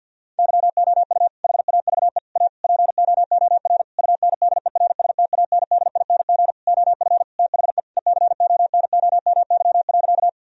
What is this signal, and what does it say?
Morse code, heard as a keyed tone, about 50 wpm, 700 Hz. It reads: YOU HAVE A POOR UNDERSTANDING OF THE ECONOMY.
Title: Sentences from Top 1000 Words - Review Part 3 - 50wpm